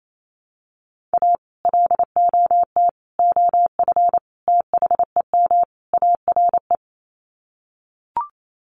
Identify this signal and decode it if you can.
Morse code, heard as a keyed tone, about 28 wpm, 700 Hz. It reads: A LOT OF THEM ARE
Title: Sets of 5 Words Encore2 28wpm